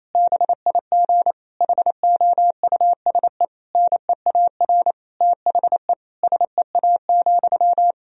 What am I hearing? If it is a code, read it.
Morse code, heard as a keyed tone, about 28 wpm, 700 Hz. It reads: BIG HOUSE NEAR THE SEA,
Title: Sentences from Top 1000 Words - Review Part 1 - 28wpm